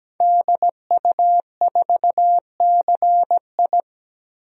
Morse code, heard as a keyed tone, about 17 wpm, 700 Hz. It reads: DU4CI
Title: DXCC-100 Call Signs - 5 Characters - Repeated 2x - 17wpm